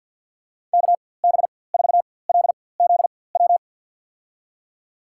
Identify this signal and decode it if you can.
Morse code, heard as a keyed tone, about 47 wpm, spaced as for 23 wpm, 700 Hz. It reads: KB4LZW
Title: USA Call Signs - 6 Characters - 4x Character Spacing - 50wpm